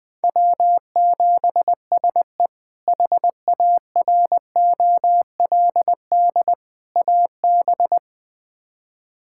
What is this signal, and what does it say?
Morse code, heard as a keyed tone, about 21 wpm, 700 Hz. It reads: W7SE HAROLD AB
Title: K1USN SST QSO Practice - 20wpm